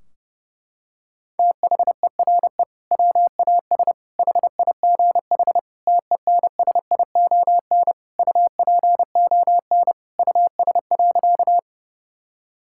Morse code, heard as a keyed tone, about 32 wpm, 700 Hz. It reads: THERE WAS HIGH TENSION UPON US.
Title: Sentences from Top 1600 Words 30wpm